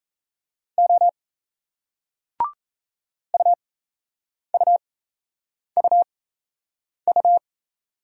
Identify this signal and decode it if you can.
Morse code, heard as a keyed tone, about 38 wpm, spaced as for 8 wpm, 700 Hz. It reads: O UUUU